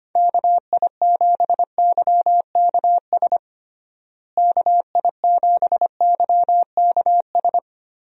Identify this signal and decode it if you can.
Morse code, heard as a keyed tone, about 25 wpm, 700 Hz. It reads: KI7YKS KI7YKS
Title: USA Call Signs - 6 Characters - Repeated 3x - 25wpm